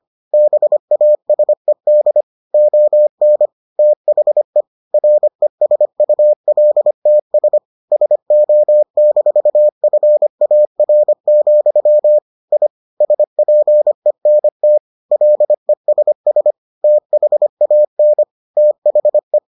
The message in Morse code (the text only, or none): BASED ON THE RESULTS SO-FAR, I SPENT LESS THAN THE